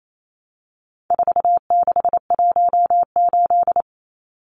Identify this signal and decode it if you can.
Morse code, heard as a keyed tone, about 28 wpm, 700 Hz. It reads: 4618
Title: Numbers - 4-Digit 28wpm